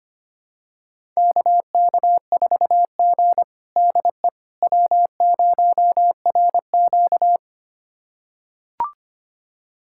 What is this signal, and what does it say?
Morse code, heard as a keyed tone, about 25 wpm, 700 Hz. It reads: KK4G DE W0RQ
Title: USA Callsign DE Callsign - 4 Characters - 25wpm